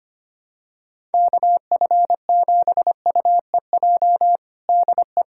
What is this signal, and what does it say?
Morse code, heard as a keyed tone, about 25 wpm, 700 Hz. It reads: KF7UEJ DE
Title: Practice QSO - Exchange 4 - Ending - 25wpm